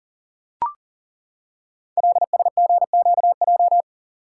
Morse code, heard as a keyed tone, about 42 wpm, 700 Hz. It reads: LSGQJ
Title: Code Groups - 5 Letter:Numbers - 40wpm